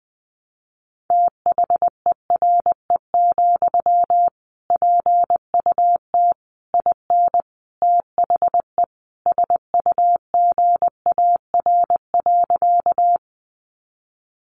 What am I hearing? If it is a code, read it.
Morse code, heard as a keyed tone, about 20 wpm, 700 Hz. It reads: THERE, PUT IN THE SUGAR.
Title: Sentences from Top 1200 Words 20wpm